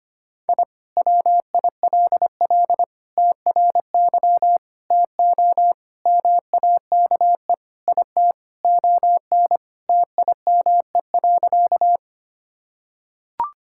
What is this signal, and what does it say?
Morse code, heard as a keyed tone, about 25 wpm, 700 Hz. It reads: I WILL TRY TO MAKE IT ON TIME.
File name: practice / 25wpm / Sentences with 3-Word Phrases - 25wpm